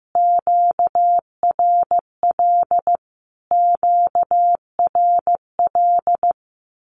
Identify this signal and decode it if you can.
Morse code, heard as a keyed tone, about 15 wpm, 700 Hz. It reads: QRL QRL
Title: Q-Codes Repeated 3x 15wpm